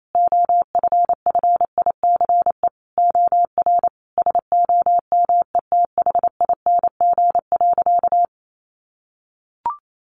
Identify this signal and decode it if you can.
Morse code, heard as a keyed tone, about 28 wpm, 700 Hz. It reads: OFFICE OR SOMETHING.
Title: Sentences from Top 1000 Words 28wpm